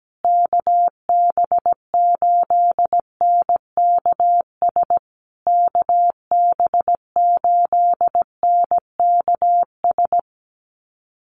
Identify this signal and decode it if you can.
Morse code, heard as a keyed tone, about 17 wpm, 700 Hz. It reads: KB8NKS KB8NKS
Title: USA Call Signs Repeated 3x 17wpm